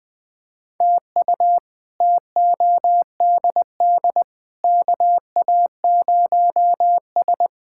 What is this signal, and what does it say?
Morse code, heard as a keyed tone, about 20 wpm, 700 Hz. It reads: TU TODD KA0S
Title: K1USN SST QSO Practice - 20wpm